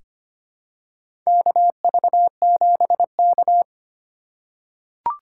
Morse code, heard as a keyed tone, about 25 wpm, 700 Hz. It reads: KV7K
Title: USA Call Signs - 2x Character Spacing - 25wpm